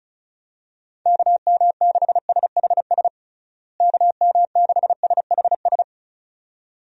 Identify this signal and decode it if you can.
Morse code, heard as a keyed tone, about 34 wpm, 700 Hz. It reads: KM6SHS KM6SHS
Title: USA Call Signs - 6 Characters - Repeated 2x - 35wpm